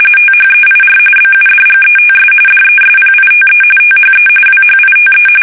RTTY, about 45.45 baud. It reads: THEIR BUDGETS DESPITE THE WHITE
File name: fsk850 / usnavy5